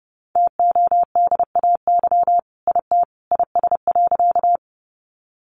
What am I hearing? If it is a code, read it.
Morse code, heard as a keyed tone, about 30 wpm, 700 Hz. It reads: TODAY IT IS.
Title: Sentences from Top 600 Words 30wpm